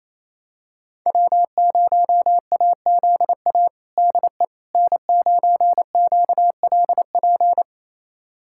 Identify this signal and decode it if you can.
Morse code, heard as a keyed tone, about 28 wpm, 700 Hz. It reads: W0AZA DE N9QLP
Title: USA Callsign DE Callsign - 5 Characters - 28wpm